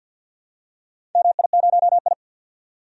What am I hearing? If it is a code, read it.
Morse code, heard as a keyed tone, about 52 wpm, 700 Hz. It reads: MI0I
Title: DXCC-100 Call Signs - 4 Characters - 50wpm